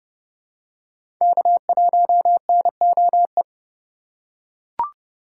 Morse code, heard as a keyed tone, about 29 wpm, 700 Hz. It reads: K1NOE
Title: USA Call Signs - 5 Characters - 3x Character Spacing - 30wpm